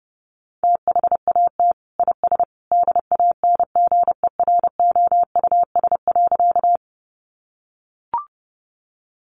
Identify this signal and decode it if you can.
Morse code, heard as a keyed tone, about 30 wpm, 700 Hz. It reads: THAT IS DANGEROUS.
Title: Sentences from Top 1200 Words 30wpm